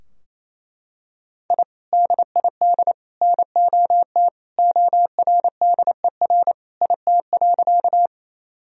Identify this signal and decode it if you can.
Morse code, heard as a keyed tone, about 27 wpm, 700 Hz. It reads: I DID NOT ORDER IT.
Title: Sentences from Top 1000 Words - Review Part 2 - 28wpm